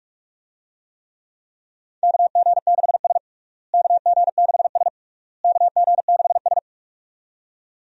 Morse code, heard as a keyed tone, about 42 wpm, 700 Hz. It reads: KG6S KG6S KG6S
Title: USA Call Signs - 4 Characters - Repeated 3x - 45wpm